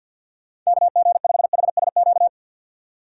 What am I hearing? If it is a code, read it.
Morse code, heard as a keyed tone, about 47 wpm, 700 Hz. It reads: KG5HSQ
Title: USA Call Signs - 4x Character Spacing - 50wpm